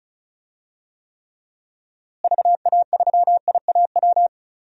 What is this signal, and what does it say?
Morse code, heard as a keyed tone, about 36 wpm, 700 Hz. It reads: VA3IAW
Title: DXCC-100 Call Signs - 6 Characters - Repeated 3x - 35wpm